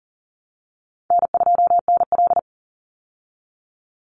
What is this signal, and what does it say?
Morse code, heard as a keyed tone, about 40 wpm, 700 Hz. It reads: N2NL
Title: Call Signs Repeated 3x - 40wpm